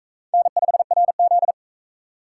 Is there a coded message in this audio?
Yes: Morse code from 0.3 to 1.5 s, heard as a keyed tone, about 42 wpm, 700 Hz. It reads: N5RZ